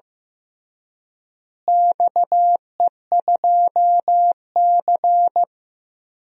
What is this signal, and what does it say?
Morse code, heard as a keyed tone, about 15 wpm, 700 Hz. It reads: XE2C
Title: DXCC-100 Call Signs - 4 Characters - Repeated 3x - 15wpm